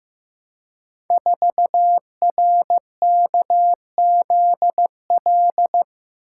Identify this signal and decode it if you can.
Morse code, heard as a keyed tone, about 15 wpm, 700 Hz. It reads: 4RKZL
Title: Code Groups - 5 Letter:Numbers - 15wpm